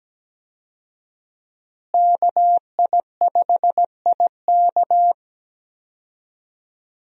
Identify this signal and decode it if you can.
Morse code, heard as a keyed tone, about 17 wpm, 700 Hz. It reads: KI5IK
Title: USA Call Signs - 5 Characters - Repeated 2x - 17wpm